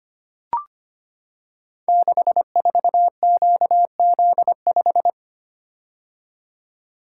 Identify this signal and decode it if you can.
Morse code, heard as a keyed tone, about 25 wpm, 700 Hz. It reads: 64QZ5
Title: Code Groups - 5 Letter:Numbers - 25wpm